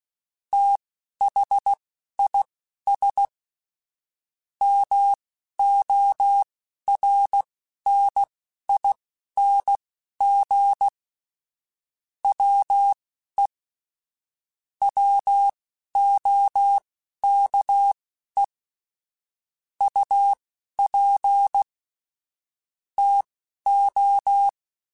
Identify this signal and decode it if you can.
Morse code, heard as a keyed tone, about 16 wpm, spaced as for 12 wpm, 800 Hz. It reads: THIS MORNING WE WOKE UP TO
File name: Weather Sets / Weather Sets-mp3s / Weather 3 17 @10